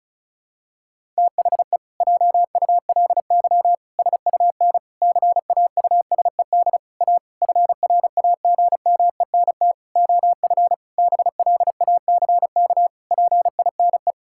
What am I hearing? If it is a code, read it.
Morse code, heard as a keyed tone, about 36 wpm, 700 Hz. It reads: THE JULY SUN CAUSED A FRAGMENT OF BLACK PINE
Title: Pangram Challenge v2 - 35wpm